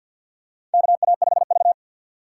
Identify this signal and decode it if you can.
Morse code, heard as a keyed tone, about 51 wpm, 690 Hz. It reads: KA5V
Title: USA Call Signs - 4 Characters - 6x Character Spacing - 50wpm